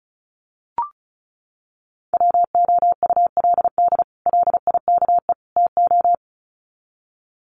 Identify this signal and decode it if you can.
Morse code, heard as a keyed tone, about 34 wpm, 700 Hz. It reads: WOULD LIKE TO
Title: Sets of 3 Words 35wpm